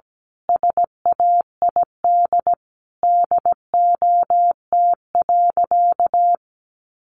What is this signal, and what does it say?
Morse code, heard as a keyed tone, about 17 wpm, 700 Hz. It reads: SAID DOT.
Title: Sentences from Top 1500 Words 17wpm